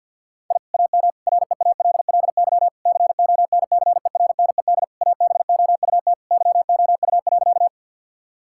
Morse code, heard as a keyed tone, about 50 wpm, 700 Hz. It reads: I AM REALLY CONCERNED ABOUT YOU.
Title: Sentences from Top 1000 Words - Review Part 6 - 50wpm